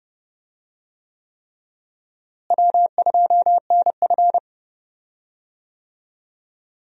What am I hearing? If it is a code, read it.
Morse code, heard as a keyed tone, about 30 wpm, 700 Hz. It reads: W2NF
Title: USA Call Signs - 4 Characters - Repeated 3x - 30wpm